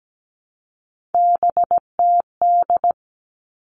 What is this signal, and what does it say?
Morse code, heard as a keyed tone, about 17 wpm, 700 Hz. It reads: BTD
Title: All Cut Numbers - 3-Digit 17wpm